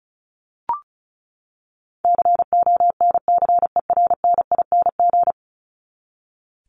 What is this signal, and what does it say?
Morse code, heard as a keyed tone, about 34 wpm, 700 Hz. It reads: CONCERNING
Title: Top 1751-2000 Words 35wpm